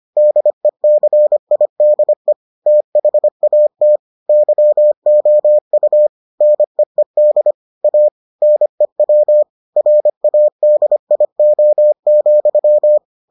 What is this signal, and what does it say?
Morse code, heard as a keyed tone, about 25 wpm, 600 Hz. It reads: DECIDE THAT YOU NEED A NEW RADIO,